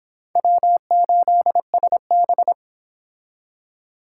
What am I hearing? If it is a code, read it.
Morse code, heard as a keyed tone, about 28 wpm, 700 Hz. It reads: W8SB